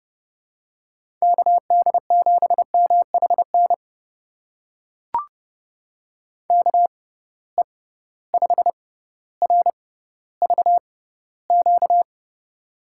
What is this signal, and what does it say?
Morse code, heard as a keyed tone, about 31 wpm, spaced as for 16 wpm, 700 Hz. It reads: KD7MHN KE5RVQ
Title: USA Call Signs - 6 Characters - 6x Character Spacing - 30wpm